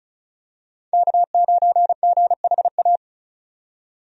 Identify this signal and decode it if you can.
Morse code, heard as a keyed tone, about 34 wpm, 700 Hz. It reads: K9GHA
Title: USA Call Signs - 5 Characters - 35wpm